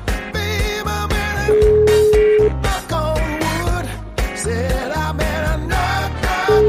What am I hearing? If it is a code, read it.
Ringback tone (Europe, 425 Hz), with noise.